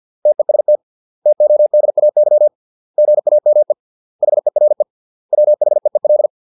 Morse code, heard as a keyed tone, about 50 wpm, 600 Hz. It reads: TEST TODAY GAME HERE WHEEL